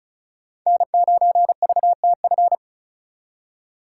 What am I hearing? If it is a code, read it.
Morse code, heard as a keyed tone, about 44 wpm, 700 Hz. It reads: N9VTF
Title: Speed Racing - USA 1x3 Call Signs 35wpm